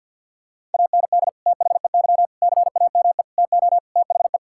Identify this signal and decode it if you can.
Morse code, heard as a keyed tone, about 50 wpm, 700 Hz. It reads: AND THEY CAME TO THE
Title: Sentences from Top 1600 Words 50wpm